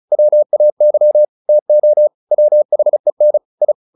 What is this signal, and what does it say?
Morse code, heard as a keyed tone, about 34 wpm, 600 Hz. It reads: WAY TO WHEN I